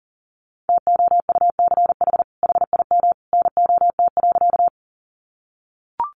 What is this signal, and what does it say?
Morse code, heard as a keyed tone, about 40 wpm, 700 Hz. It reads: TOUCH HIM NOT.
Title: Sentences from Top 1200 Words 40wpm